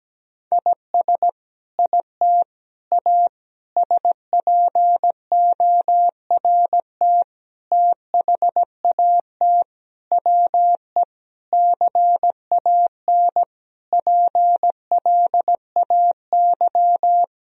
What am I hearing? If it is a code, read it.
Morse code, heard as a keyed tone, about 17 wpm, 700 Hz. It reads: IS IT A SPORT THAT WE CAN PLAY
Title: Sentences from Top 1000 Words - Review Part 4 - 17wpm